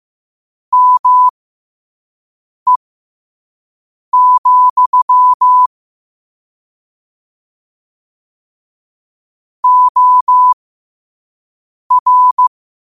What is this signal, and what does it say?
Morse code, heard as a keyed tone, about 15 wpm, spaced as for 5 wpm, 1000 Hz. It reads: ME, OR
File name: qotdfiles / 5wpm